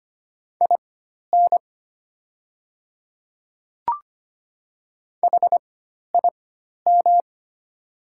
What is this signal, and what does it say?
Morse code, heard as a keyed tone, about 25 wpm, spaced as for 12 wpm, 690 Hz. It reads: IN HIM
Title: 4x Character Spacing - Top 100 Words - 25wpm